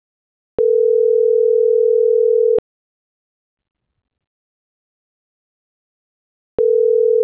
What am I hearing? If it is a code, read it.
Ringback tone (North America), 440 + 480 Hz.